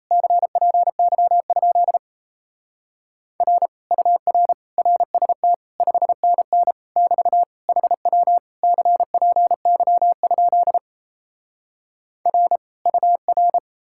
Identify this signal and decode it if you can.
Morse code, heard as a keyed tone, about 38 wpm, 700 Hz. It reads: CPY? R UR RST 5NN <BT> HW CPY? R UR